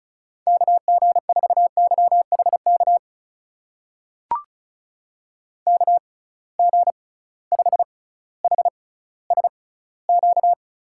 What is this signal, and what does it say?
Morse code, heard as a keyed tone, about 36 wpm, spaced as for 18 wpm, 700 Hz. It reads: KG4YHK KG5HSQ
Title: USA Call Signs - 6 Characters - 6x Character Spacing - 35wpm